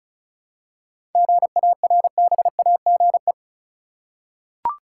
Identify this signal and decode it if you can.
Morse code, heard as a keyed tone, about 34 wpm, 700 Hz. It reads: GARBAGE
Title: Seven Letter Words - 35wpm